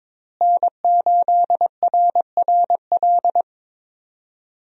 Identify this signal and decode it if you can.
Morse code, heard as a keyed tone, about 21 wpm, 700 Hz. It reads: N8RRL
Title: USA Call Signs - 5 Characters - Repeated 2x - 22wpm